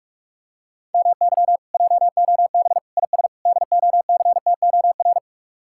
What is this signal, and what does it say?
Morse code, heard as a keyed tone, about 48 wpm, 700 Hz. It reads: MY JOB IS DOCTOR